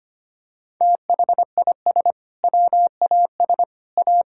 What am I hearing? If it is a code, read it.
Morse code, heard as a keyed tone, about 25 wpm, 700 Hz. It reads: THIS WAS A
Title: Sentences from Top 1000 Words - Review Part 5 - 25wpm